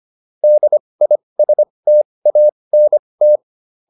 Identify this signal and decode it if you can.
Morse code, heard as a keyed tone, about 25 wpm, spaced as for 20 wpm, 600 Hz. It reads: DISTANT
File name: DIS7 20